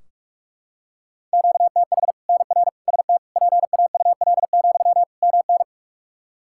Morse code, heard as a keyed tone, about 46 wpm, 700 Hz. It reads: QTH NR ST PAUL, MN
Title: K1USN SST QSO Practice 45wpm